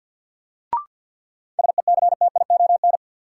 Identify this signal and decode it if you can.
Morse code, heard as a keyed tone, about 52 wpm, 700 Hz. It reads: SECTION
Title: Sentences from Top 1000 Words - Review Part 5 - New Word Drill - 50wpm